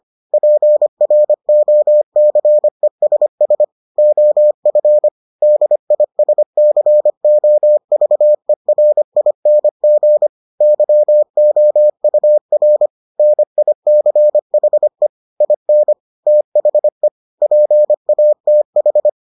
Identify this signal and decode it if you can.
Morse code, heard as a keyed tone, about 25 wpm, 600 Hz. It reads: PROCESS OF DISCOVERING YOUR NICHE IN THE PATH